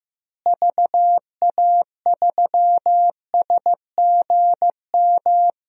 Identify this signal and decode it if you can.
Morse code, heard as a keyed tone, about 15 wpm, 700 Hz. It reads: VA3SGM
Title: DXCC-100 Call Signs - 6 Characters - Repeated 2x - 15wpm